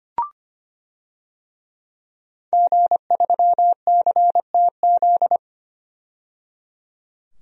Morse code, heard as a keyed tone, about 25 wpm, 700 Hz. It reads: G3CTZ
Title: DXCC-100 Call Signs - 5 Characters - 25wpm